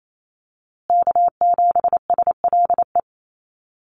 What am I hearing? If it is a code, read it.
Morse code, heard as a keyed tone, about 28 wpm, 700 Hz. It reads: K7SLE
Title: USA Call Signs Repeated 3x 28wpm